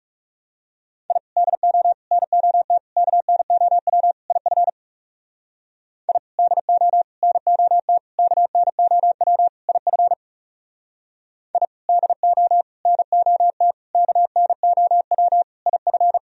Morse code, heard as a keyed tone, about 40 wpm, 700 Hz. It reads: I DO NOT KNOW IF I DO NOT KNOW IF I DO NOT KNOW IF